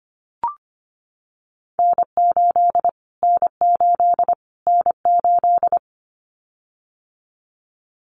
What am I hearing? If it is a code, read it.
Morse code, heard as a keyed tone, about 25 wpm, 700 Hz. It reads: N8 N8 N8
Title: Callsign Prefixes for Top 25 DXCC - Repeated 3x - 25wpm